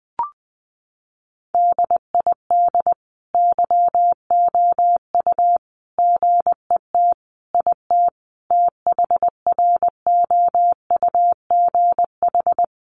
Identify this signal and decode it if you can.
Morse code, heard as a keyed tone, about 20 wpm, 700 Hz. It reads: DID YOU GET IT THROUGH
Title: Sentences from Top 1300 Words 20wpm